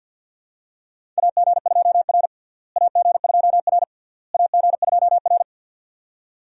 Morse code, heard as a keyed tone, about 50 wpm, 700 Hz. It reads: AG2R AG2R AG2R
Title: USA Call Signs - 4 Characters - Repeated 3x - 50wpm